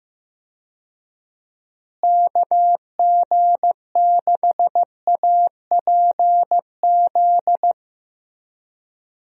Morse code, heard as a keyed tone, about 15 wpm, 700 Hz. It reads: KG6APZ
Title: USA Call Signs - 6 Characters - Repeated 3x - 15wpm